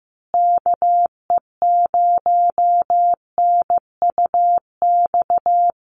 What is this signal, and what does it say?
Morse code, heard as a keyed tone, about 15 wpm, 700 Hz. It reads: KE0NUX
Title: USA Call Signs Repeated 3x 15wpm